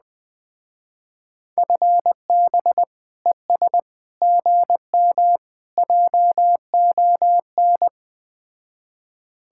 Morse code, heard as a keyed tone, about 20 wpm, 700 Hz. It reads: FB ES GM JON
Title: Speed-Racing Common QSO Phrases Encore - 17wpm